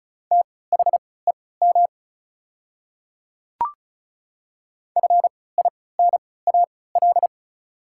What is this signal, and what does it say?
Morse code, heard as a keyed tone, about 34 wpm, spaced as for 20 wpm, 690 Hz. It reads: THEM FINAL
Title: 3x Character Spacing - Top 500 Words - 35wpm